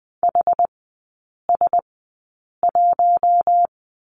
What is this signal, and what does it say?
Morse code, heard as a keyed tone, about 20 wpm, spaced as for 8 wpm, 700 Hz. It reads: HS1
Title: 138 Lesson 23e - Letter f - Three Characters 20wpm